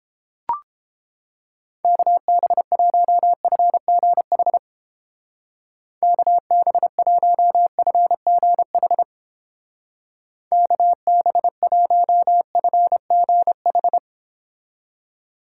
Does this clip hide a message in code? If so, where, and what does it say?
Yes: Morse code from 1.8 to 14.0 s, heard as a keyed tone, about 31 wpm, 700 Hz. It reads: KB1FGH KB1FGH KB1FGH